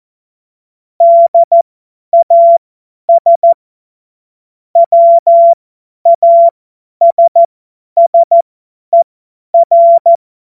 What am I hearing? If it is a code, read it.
Morse code, heard as a keyed tone, about 14 wpm, spaced as for 10 wpm, 680 Hz. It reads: DAS WASSER
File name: Morsekurs / Witze / Witz 01. März 2023 50 F70